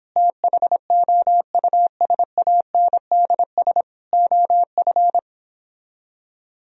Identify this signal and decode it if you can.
Morse code, heard as a keyed tone, about 29 wpm, 700 Hz. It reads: THOUSANDS OF